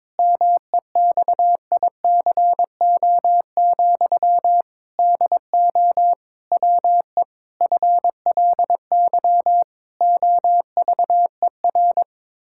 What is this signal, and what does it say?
Morse code, heard as a keyed tone, about 22 wpm, 700 Hz. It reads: MEXICO, DO WE FLY OVER
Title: Pangram Challenge v2 - 22wpm